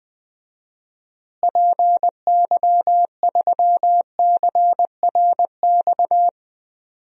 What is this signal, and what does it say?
Morse code, heard as a keyed tone, about 20 wpm, 700 Hz. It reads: PY3CRX
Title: DXCC-100 Call Signs - 6 Characters - Repeated 3x - 20wpm